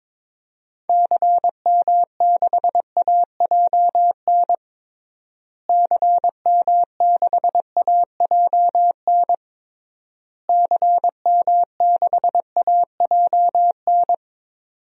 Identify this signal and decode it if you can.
Morse code, heard as a keyed tone, about 21 wpm, 700 Hz. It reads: CM6AJN CM6AJN CM6AJN
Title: DXCC-100 Call Signs - 6 Characters - Repeated 3x - 22wpm